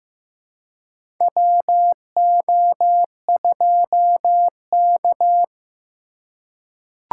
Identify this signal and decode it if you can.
Morse code, heard as a keyed tone, about 16 wpm, 700 Hz. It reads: WO2K
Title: USA Call Signs - 4x Character Spacing - 15wpm